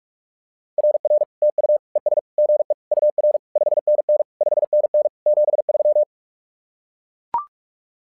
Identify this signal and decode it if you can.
Morse code, heard as a keyed tone, about 44 wpm, 600 Hz. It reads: RR TU ES GE UR 5NN 5NN 73
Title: SOTA QSO Practice 45wpm